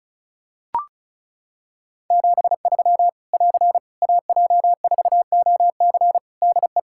Morse code, heard as a keyed tone, about 36 wpm, 700 Hz. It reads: 73 <AR> AJ4OC DE
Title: Practice QSO - Exchange 4 - Ending - 35wpm